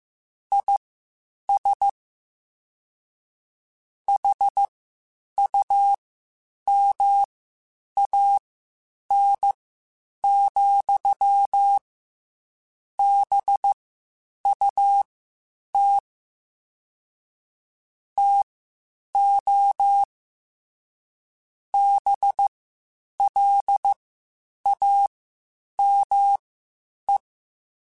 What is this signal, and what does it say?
Morse code, heard as a keyed tone, about 15 wpm, spaced as for 8 wpm, 800 Hz. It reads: IS HUMAN, BUT TO BLAME